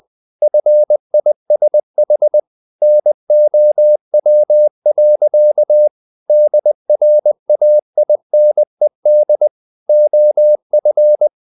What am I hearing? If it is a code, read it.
Morse code, heard as a keyed tone, about 20 wpm, 600 Hz. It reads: FISH NOW. DRAINED OF
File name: the old man and the sea part 2 20wpm